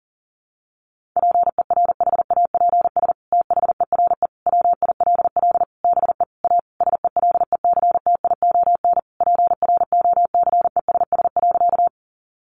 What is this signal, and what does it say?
Morse code, heard as a keyed tone, about 37 wpm, 700 Hz. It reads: PERHAPS THERE WILL BE A SELECTION PROCESS.
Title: Sentences from Top 1400 Words 40wpm